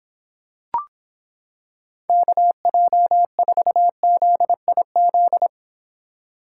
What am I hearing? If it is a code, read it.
Morse code, heard as a keyed tone, about 26 wpm, 700 Hz. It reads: KJ4ZIZ